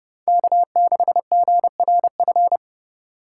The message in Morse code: K6GRF

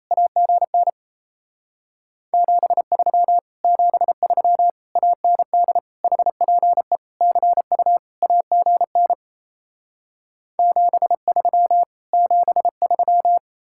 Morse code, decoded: AGN 73 73 AND HPE CU AGN 73 73

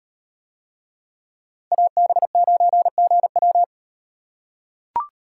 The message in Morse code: AB9GW

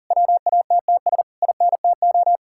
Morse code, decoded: WATTS INTO

